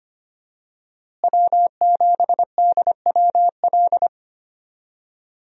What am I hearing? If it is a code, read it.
Morse code, heard as a keyed tone, about 25 wpm, 700 Hz. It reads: W7DWL
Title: USA Call Signs - 5 Characters - Repeated 2x - 25wpm